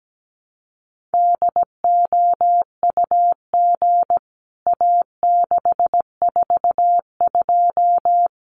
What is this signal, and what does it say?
Morse code, heard as a keyed tone, about 17 wpm, 700 Hz. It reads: DOUG A642
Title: CWT Exchange 17wpm